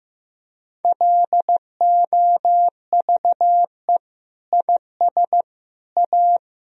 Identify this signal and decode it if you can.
Morse code, heard as a keyed tone, about 15 wpm, 700 Hz. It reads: LOVE IS A
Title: Sentences from Top 1000 Words - Review Part 4 - 15wpm